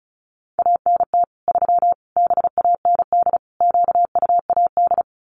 Morse code, heard as a keyed tone, about 34 wpm, 700 Hz. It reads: ANT 3 BAND QUAD
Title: Common QSO Phrases 35wpm